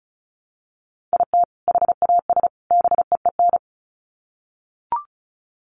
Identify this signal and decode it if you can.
Morse code, heard as a keyed tone, about 34 wpm, 700 Hz. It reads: IT HAS BEEN
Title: Sets of 3 Words 35wpm